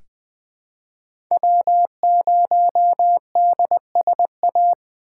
Morse code, heard as a keyed tone, about 20 wpm, 700 Hz. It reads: W0DSA
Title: USA Call Signs - 4x Character Spacing - 20wpm